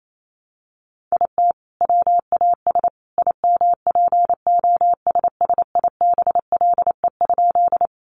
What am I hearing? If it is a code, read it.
Morse code, heard as a keyed tone, about 28 wpm, 700 Hz. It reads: IT WAS IMPOSSIBLE?
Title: Sentences from Top 1100 Words 28wpm